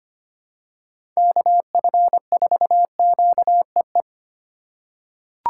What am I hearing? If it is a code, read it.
Morse code, heard as a keyed tone, about 25 wpm, 700 Hz. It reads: KF4QEE
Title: USA Call Signs - 6 Characters - 2x Character Spacing - 25wpm